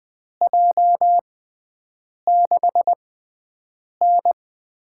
Morse code, heard as a keyed tone, about 20 wpm, spaced as for 7 wpm, 700 Hz. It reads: J6N
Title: USA Call Signs - 6 Characters - 6x Character Spacing - 20wpm